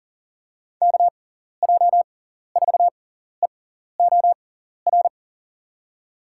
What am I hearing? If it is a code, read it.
Morse code, heard as a keyed tone, about 40 wpm, spaced as for 14 wpm, 700 Hz. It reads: KJ4EOR
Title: USA Call Signs - 6 Characters - 6x Character Spacing - 40wpm